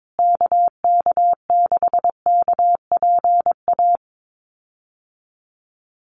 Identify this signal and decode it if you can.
Morse code, heard as a keyed tone, about 22 wpm, 700 Hz. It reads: KK6KPA
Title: USA Call Signs Repeated 3x 22wpm